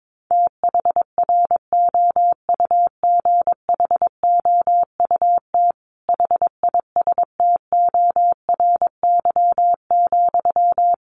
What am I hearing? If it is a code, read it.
Morse code, heard as a keyed tone, about 22 wpm, 700 Hz. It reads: THROUGHOUT HISTORY,
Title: Sentences from Top 800 Words 22wpm